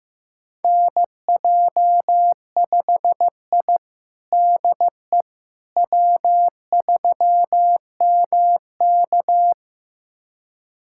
Morse code, heard as a keyed tone, about 15 wpm, 700 Hz. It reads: NJ5I DE W3MK
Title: USA Callsign DE Callsign - 4 Characters - 15wpm